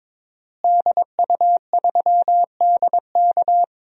Morse code, heard as a keyed tone, about 22 wpm, 700 Hz. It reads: DU3DK
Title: DXCC-100 Call Signs - 5 Characters - Repeated 3x - 22wpm